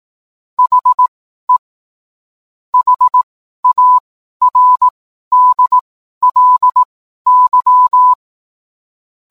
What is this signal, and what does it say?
Morse code, heard as a keyed tone, about 18 wpm, spaced as for 12 wpm, 1000 Hz. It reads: HE HARDLY